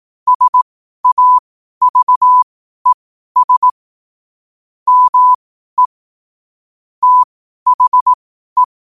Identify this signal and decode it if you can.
Morse code, heard as a keyed tone, about 18 wpm, spaced as for 12 wpm, 1000 Hz. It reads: SAVES ME THE